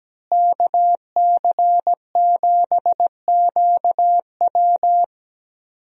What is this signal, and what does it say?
Morse code, heard as a keyed tone, about 17 wpm, 700 Hz. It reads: KC7QW
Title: USA Call Signs - 5 Characters - Repeated 3x - 17wpm